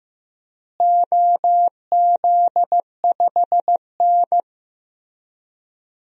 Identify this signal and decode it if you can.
Morse code, heard as a keyed tone, about 15 wpm, 700 Hz. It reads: OZ5N
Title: DXCC-100 Call Signs - 4 Characters - Repeated 3x - 15wpm